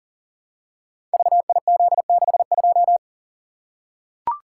Morse code, heard as a keyed tone, about 40 wpm, 700 Hz. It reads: VIZ62
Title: Code Groups - 5 Letter:Numbers - 40wpm